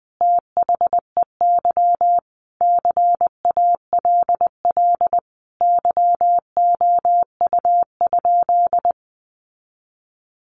Morse code, heard as a keyed tone, about 20 wpm, 700 Hz. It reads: THEY CALL YOU?
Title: Sentences from Top 100 Words 20wpm